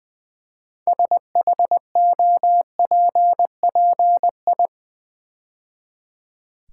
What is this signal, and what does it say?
Morse code, heard as a keyed tone, about 20 wpm, 700 Hz. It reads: SHOPPI
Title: Build A Word - 2 Letters At A Time - 8-Letter Words - 20wpm